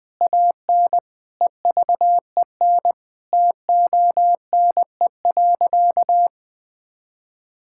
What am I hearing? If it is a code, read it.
Morse code, heard as a keyed tone, about 20 wpm, 700 Hz. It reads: AN EVEN TONE.
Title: Sentences from Top 1600 Words 20wpm